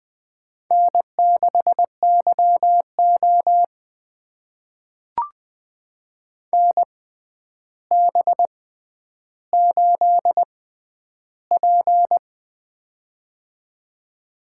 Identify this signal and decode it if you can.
Morse code, heard as a keyed tone, about 20 wpm, spaced as for 10 wpm, 700 Hz. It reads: N6YO NB8P
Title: USA Call Signs - 4 Characters - 6x Character Spacing - 20wpm